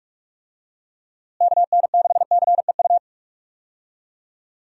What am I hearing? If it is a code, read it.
Morse code, heard as a keyed tone, about 46 wpm, 700 Hz. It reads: KN6CEU
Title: USA Call Signs - 6 Characters - Repeated 2x - 45wpm